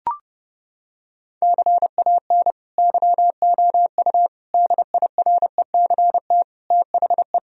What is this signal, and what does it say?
Morse code, heard as a keyed tone, about 30 wpm, 700 Hz. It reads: CAN YOU DIRECT THE
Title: Sentences from Top 1000 Words - Review Part 2 - 30wpm